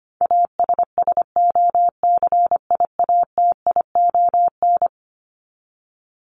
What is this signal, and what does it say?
Morse code, heard as a keyed tone, about 25 wpm, 700 Hz. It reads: ASSOCIATION
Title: Top 1001-1250 Words 25wpm